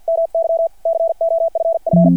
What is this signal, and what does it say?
Morse code, heard as a keyed tone, about 50 wpm, 660 Hz. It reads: MY COFF
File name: archived / CRSnet